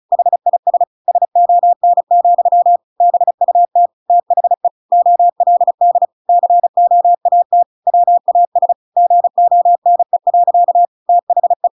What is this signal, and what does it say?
Morse code, heard as a keyed tone, about 34 wpm, 700 Hz. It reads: HIS SON, BUT THE OLD COAT WAS GONE. THE
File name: Pinocchio / Pinocchio ch8 35 35 ik1hge